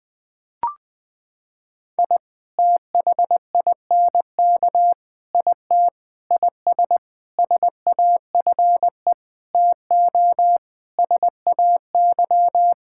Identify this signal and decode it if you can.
Morse code, heard as a keyed tone, about 20 wpm, 700 Hz. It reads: I THINK IT IS SAFE TO SAY
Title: Sentences with 8-Word Phrases - 20wpm